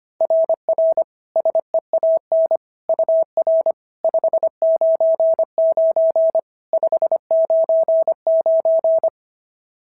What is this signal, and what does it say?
Morse code, heard as a keyed tone, about 25 wpm, 650 Hz. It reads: RR SEAN UR 599 599
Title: SOTA QSO Practice 25wpm